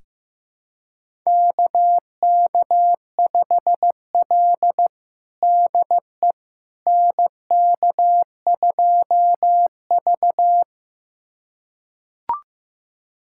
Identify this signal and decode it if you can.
Morse code, heard as a keyed tone, about 15 wpm, 700 Hz. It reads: KK5L DE NK2V
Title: USA Callsign DE Callsign - 4 Characters - 15wpm